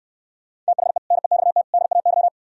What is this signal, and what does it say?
Morse code, heard as a keyed tone, about 67 wpm, 700 Hz. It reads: THE NEXT DAY